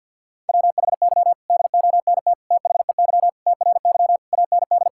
Morse code, heard as a keyed tone, about 50 wpm, 700 Hz. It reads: WHY DONT THEY TRY AND
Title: Sentences from Top 1000 Words - Review Part 5 - 50wpm